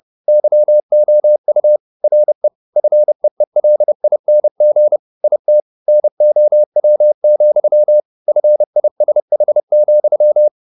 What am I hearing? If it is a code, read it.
Morse code, heard as a keyed tone, about 30 wpm, 600 Hz. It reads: YOU RE FEELING IT NOW, FISH,
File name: the old man and the sea / the old man and the sea 1 30wpm